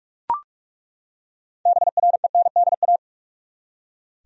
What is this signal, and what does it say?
Morse code, heard as a keyed tone, about 41 wpm, 700 Hz. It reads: BRENDA
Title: USA Common Names - 45wpm